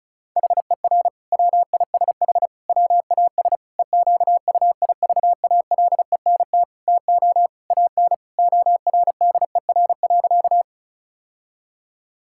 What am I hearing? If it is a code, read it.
Morse code, heard as a keyed tone, about 34 wpm, 700 Hz. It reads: HER WISH WAS EQUIVALENT TO AN ORDER.
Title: Sentences from Top 1600 Words 35wpm